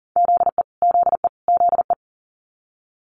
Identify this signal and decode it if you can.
Morse code, heard as a keyed tone, about 39 wpm, 700 Hz. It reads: ZE ZE ZE
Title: Multi-letter Phonemes Repeated 3x 40wpm